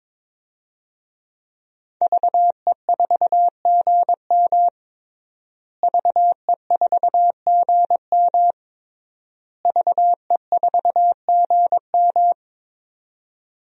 Morse code, heard as a keyed tone, about 22 wpm, 700 Hz. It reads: VE4GM VE4GM VE4GM
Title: DXCC-100 Call Signs - 5 Characters - Repeated 3x - 22wpm